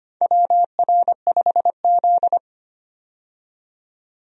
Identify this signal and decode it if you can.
Morse code, heard as a keyed tone, about 25 wpm, 700 Hz. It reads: WR5Z
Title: USA Call Signs - 4 Characters - 25wpm